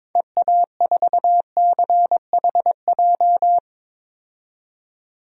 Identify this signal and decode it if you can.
Morse code, heard as a keyed tone, about 22 wpm, 700 Hz. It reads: EA4CHJ
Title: DXCC-100 Call Signs - 6 Characters - 22wpm